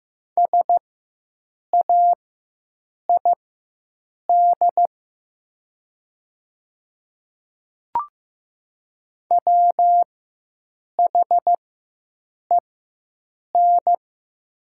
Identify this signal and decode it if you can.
Morse code, heard as a keyed tone, about 15 wpm, spaced as for 7 wpm, 700 Hz. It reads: SAID WHEN
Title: 4x Character Spacing - Top 300 Words - 15wpm